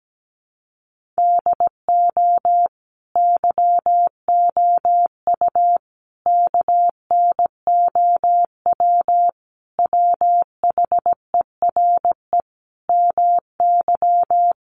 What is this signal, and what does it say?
Morse code, heard as a keyed tone, about 17 wpm, 700 Hz. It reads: DO YOU KNOW WHERE MY
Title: Sentences from Top 1400 Words 17wpm